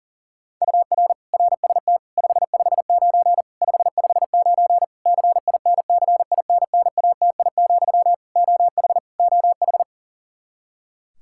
Morse code, heard as a keyed tone, about 40 wpm, 700 Hz. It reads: UR RST 559 559 CINCINNATI, OH OH
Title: Practice QSO - Exchange 1 - Beginning - 40wpm